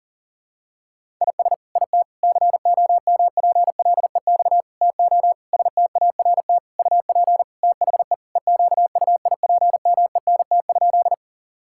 Morse code, heard as a keyed tone, about 40 wpm, 700 Hz. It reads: IS IT COMPLEX TO START UP THE EQUIPMENT?
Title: Sentences from Top 1000 Words - Review Part 6 - 40wpm